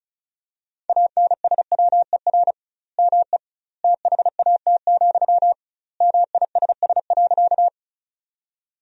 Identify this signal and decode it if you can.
Morse code, heard as a keyed tone, about 34 wpm, 700 Hz. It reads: ANSWER ME THAT, MISS.
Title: 2x Word-Spacing - Sentences from Top 500 Words - 35wpm